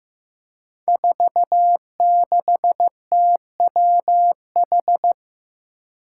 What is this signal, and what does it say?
Morse code, heard as a keyed tone, about 15 wpm, 700 Hz. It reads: 46TWH
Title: Five Challenging Letter-Number Combinations 15wpm